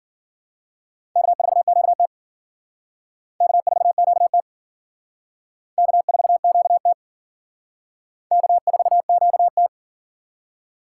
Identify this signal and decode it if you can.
Morse code, heard as a keyed tone, about 50 wpm, 700 Hz. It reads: K4QT K4QT K4QT K4QT
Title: Speed Racing - USA 1x2 Call Signs 40wpm